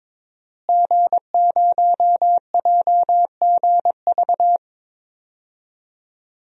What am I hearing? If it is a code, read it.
Morse code, heard as a keyed tone, about 22 wpm, 700 Hz. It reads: G0JGV
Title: DXCC-100 Call Signs - 5 Characters - Repeated 3x - 22wpm